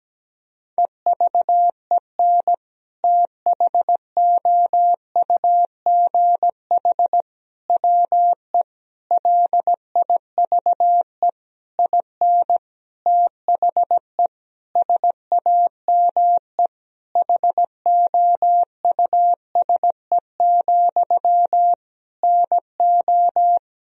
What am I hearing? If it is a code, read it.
Morse code, heard as a keyed tone, about 17 wpm, 700 Hz. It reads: EVEN THOUGH WE LIVE IN THE SAME HOUSE, NO
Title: Sentences from Top 1000 Words - Review Part 2 - 17wpm